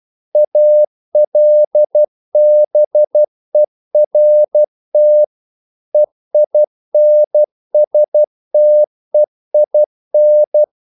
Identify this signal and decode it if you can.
Morse code, heard as a keyed tone, about 12 wpm, 600 Hz. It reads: ALBERT EINSTEIN
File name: CW / Practice / EnglishQuotations0112